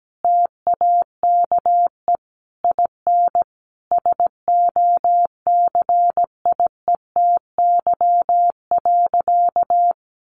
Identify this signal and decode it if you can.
Morse code, heard as a keyed tone, about 17 wpm, 700 Hz. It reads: TAKE IN SOCIETY.
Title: Sentences from Top 1500 Words 17wpm